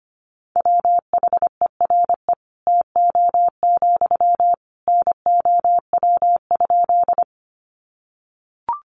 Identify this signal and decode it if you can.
Morse code, heard as a keyed tone, about 25 wpm, 700 Hz. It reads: WHERE TO, NOW?
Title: Sentences from Top 200 Words 25wpm